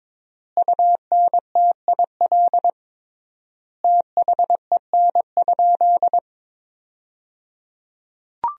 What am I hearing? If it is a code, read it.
Morse code, heard as a keyed tone, about 22 wpm, 700 Hz. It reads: UNTIL THEN?
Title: 3x Word-Spacing - Sentences from Top 500 Words - 22wpm